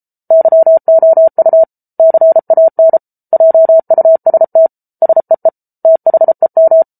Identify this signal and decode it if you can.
Morse code, heard as a keyed tone, about 34 wpm, 650 Hz. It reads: YOU CAN JUST SEE THEM